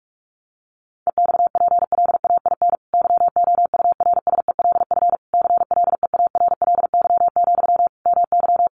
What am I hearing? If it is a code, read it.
Morse code, heard as a keyed tone, about 44 wpm, 700 Hz. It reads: EXPLAIN YOURSELF CLEARLY, MY
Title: Sentences from Top 1300 Words 45wpm